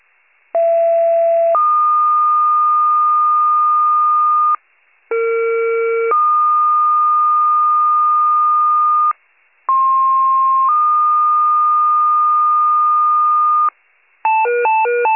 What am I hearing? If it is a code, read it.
Two-tone page, heard as a one-second tone, then a three-second tone, about 670 then 1220 Hz, about 460 then 1220 Hz, about 1060 then 1220 Hz. It is heard from 0.5 to 13.7 s.
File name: pager / 21 RESCUE / wcalert